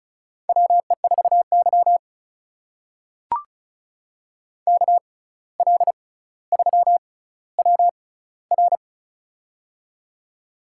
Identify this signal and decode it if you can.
Morse code, heard as a keyed tone, about 36 wpm, spaced as for 17 wpm, 700 Hz. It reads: WE4Y KL3WR
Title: USA Call Signs - 6x Character Spacing - 35wpm